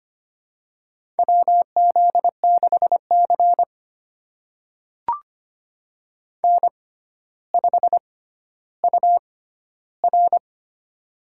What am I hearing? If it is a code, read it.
Morse code, heard as a keyed tone, about 25 wpm, spaced as for 13 wpm, 700 Hz. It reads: WZ6C N5UR
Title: USA Call Signs - 4 Characters - 6x Character Spacing - 25wpm